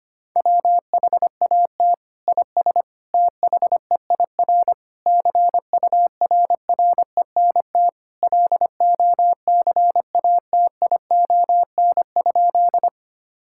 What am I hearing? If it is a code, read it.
Morse code, heard as a keyed tone, about 25 wpm, 700 Hz. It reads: WHAT IS THEIR CURRENT LOCATION?
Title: Sentences from Top 1000 Words - Review Part 6 - 25wpm